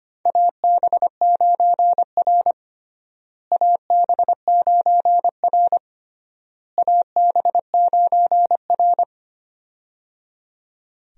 Morse code, heard as a keyed tone, about 25 wpm, 700 Hz. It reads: AB9R AB9R AB9R
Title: USA Call Signs - 4 Characters - Repeated 3x - 25wpm